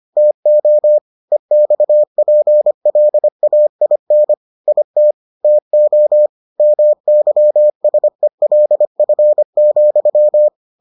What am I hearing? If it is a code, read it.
Morse code, heard as a keyed tone, about 25 wpm, 600 Hz. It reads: TO EXPLAIN IT TO MYSELF,